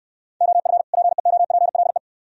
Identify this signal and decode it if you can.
Morse code, heard as a keyed tone, about 69 wpm, 700 Hz. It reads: OF PEOPLE